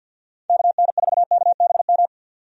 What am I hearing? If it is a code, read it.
Morse code, heard as a keyed tone, about 51 wpm, 700 Hz. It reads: KN4KBM
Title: USA Call Signs - 6 Characters - 50wpm